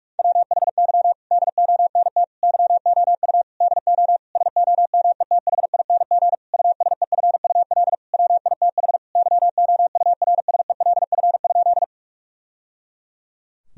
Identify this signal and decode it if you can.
Morse code, heard as a keyed tone, about 46 wpm, 700 Hz. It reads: WHY DONT YOU DO SOMETHING USEFUL WITH YOURSELF?
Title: Sentences from Top 1000 Words - Review Part 4 - 45wpm